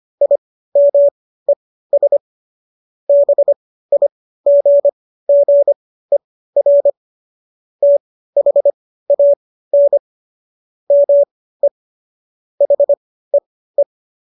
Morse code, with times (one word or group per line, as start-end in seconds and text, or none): 0.2-2.2 IMES
3.1-6.9 BIGGER
7.8-10.0 THAN
10.9-11.7 ME
12.6-13.8 HEE